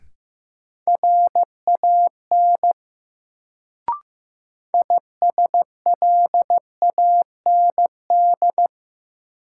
Morse code, heard as a keyed tone, about 15 wpm, 700 Hz. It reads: RAN ISLAND
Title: Sentences from Top 500 Words - New Word Drill - 15wpm